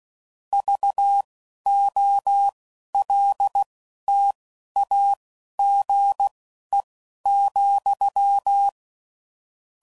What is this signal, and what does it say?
Morse code, heard as a keyed tone, about 16 wpm, spaced as for 11 wpm, 800 Hz. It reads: VOLTAGE,
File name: CW Code Groups @10WPM / Technical Subjects / XL